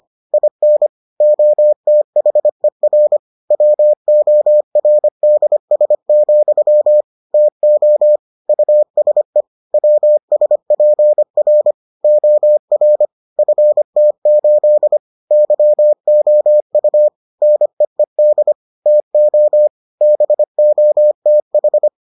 Morse code, heard as a keyed tone, about 25 wpm, 600 Hz. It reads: IN OTHER WORDS, TO USE WSPR OR FT8 YOU NEED TO BOTH